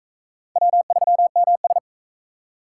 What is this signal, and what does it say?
Morse code, heard as a keyed tone, about 36 wpm, 700 Hz. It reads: W3MS